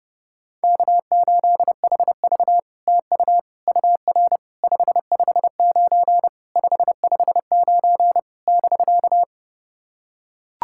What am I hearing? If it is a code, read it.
Morse code, heard as a keyed tone, about 28 wpm, 690 Hz. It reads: K8HV TU UR 559 559 <BK>
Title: SOTA QSO Practice 30wpm